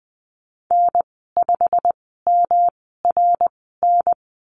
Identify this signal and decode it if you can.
Morse code, heard as a keyed tone, about 20 wpm, spaced as for 15 wpm, 700 Hz. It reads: N5MRN
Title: 147 Lesson 24g - Letter y - Call Signs 20wpm